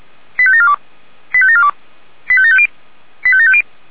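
Five-tone selective call (ZVEI), heard as quick runs of five tones, tones of about 70 ms. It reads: 85742 85742 85750 85750
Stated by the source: ZVEI five-tone selective call received on an SDR, from the Feuerwehr Orlamünde audio server.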